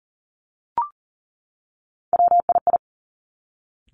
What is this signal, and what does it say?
Morse code, heard as a keyed tone, about 42 wpm, 700 Hz. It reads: WII
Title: Three Challenging Letter-Number Combinations 40wpm